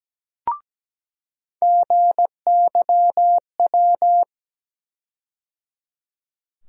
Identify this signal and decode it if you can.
Morse code, heard as a keyed tone, about 17 wpm, 700 Hz. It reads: GYW
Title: USA Call Sign Postfixes - 3 Letters 17wpm